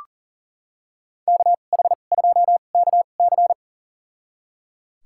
Morse code, heard as a keyed tone, about 37 wpm, spaced as for 28 wpm, 700 Hz. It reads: KH2KC
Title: USA Call Signs - 5 Characters - 2x Character Spacing - 40wpm